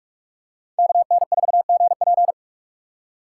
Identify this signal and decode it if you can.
Morse code, heard as a keyed tone, about 45 wpm, 700 Hz. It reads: KN4GP
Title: USA Call Signs - 5 Characters - 2x Character Spacing - 45wpm